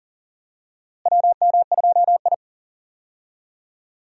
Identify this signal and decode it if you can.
Morse code, heard as a keyed tone, about 40 wpm, 700 Hz. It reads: WM2I
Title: USA Call Signs - 4 Characters - Repeated 2x - 40wpm